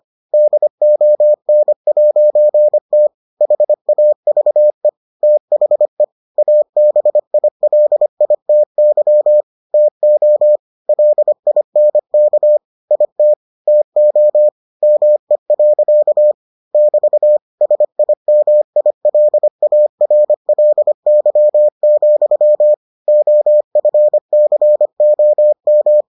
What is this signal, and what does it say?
Morse code, heard as a keyed tone, about 25 wpm, 600 Hz. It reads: DON'T HAVE THE ABILITY TO LINK IT TO ME. <BT> SIMILARLY, OFCOM